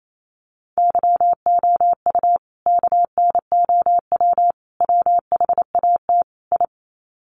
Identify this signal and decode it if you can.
Morse code, heard as a keyed tone, about 28 wpm, 700 Hz. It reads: YOU KNOW WHAT I
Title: Sets of 4 Words 28wpm